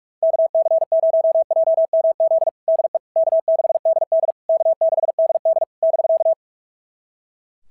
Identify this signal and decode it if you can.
Morse code, heard as a keyed tone, about 46 wpm, 690 Hz. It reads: KC0JMZ DE K6DD K6DD <BK>
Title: SOTA QSO Practice 45wpm